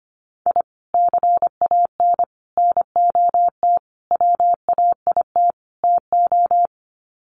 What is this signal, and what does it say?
Morse code, heard as a keyed tone, about 25 wpm, 700 Hz. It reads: I CAN NOT WAIT TO
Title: Sets of 5 Words Encore 25wpm